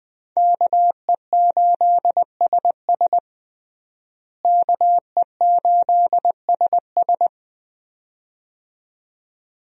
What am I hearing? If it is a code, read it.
Morse code, heard as a keyed tone, about 21 wpm, 700 Hz. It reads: KE8SS KE8SS
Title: USA Call Signs - 5 Characters - Repeated 2x - 20wpm